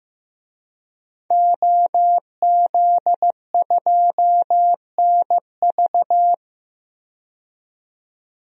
Morse code, heard as a keyed tone, about 15 wpm, 700 Hz. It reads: OZ2NV
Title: DXCC-100 Call Signs - 5 Characters - 15wpm